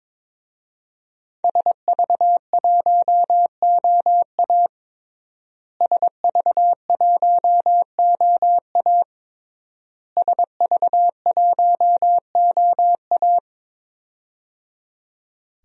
Morse code, heard as a keyed tone, about 22 wpm, 700 Hz. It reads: SV1OA SV1OA SV1OA
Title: DXCC-100 Call Signs - 5 Characters - Repeated 3x - 22wpm